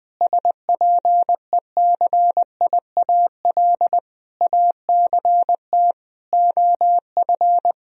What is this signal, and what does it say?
Morse code, heard as a keyed tone, about 20 wpm, 700 Hz. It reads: SPECIAL ACT OF
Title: Sentences from Top 1000 Words - Review Part 3 - 20wpm